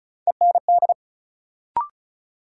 Morse code, heard as a keyed tone, about 36 wpm, 700 Hz. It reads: END
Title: Sentences from Top 200 Words - New Word Drill - 35wpm